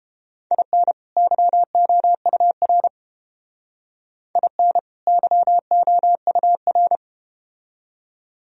Morse code, heard as a keyed tone, about 31 wpm, 700 Hz. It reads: IN YOUR IN YOUR